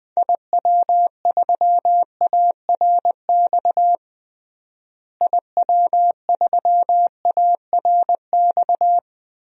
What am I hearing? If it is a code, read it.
Morse code, heard as a keyed tone, about 20 wpm, 700 Hz. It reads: IW3ARX IW3ARX
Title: DXCC-100 Call Signs - 6 Characters - Repeated 3x - 20wpm